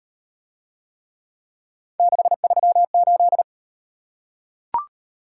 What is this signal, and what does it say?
Morse code, heard as a keyed tone, about 37 wpm, 700 Hz. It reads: 638